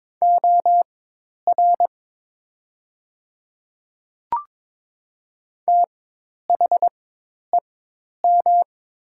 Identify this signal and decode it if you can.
Morse code, heard as a keyed tone, about 22 wpm, spaced as for 10 wpm, 700 Hz. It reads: OR THEM
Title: 4x Character Spacing - Top 100 Words - 22wpm